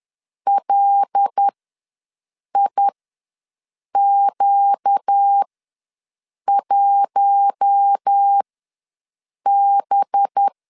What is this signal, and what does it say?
Morse code, heard as a keyed tone, about 10 wpm, spaced as for 6 wpm, 800 Hz. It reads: LIQ1B
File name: LEZIONI / 8 YJ45 / 8 Y ws 10wpm 3952